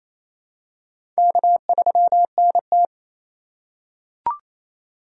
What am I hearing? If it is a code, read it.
Morse code, heard as a keyed tone, about 27 wpm, 700 Hz. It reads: K3NT
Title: USA Call Signs - 3x Character Spacing - 28wpm